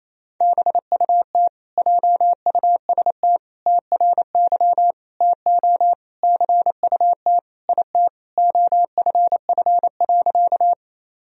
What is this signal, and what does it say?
Morse code, heard as a keyed tone, about 28 wpm, 700 Hz. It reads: BUT JUST TRY TO CUT IT OFF.
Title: Sentences from Top 1000 Words - Review Part 1 - 28wpm